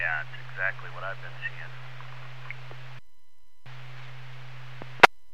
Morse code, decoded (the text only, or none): none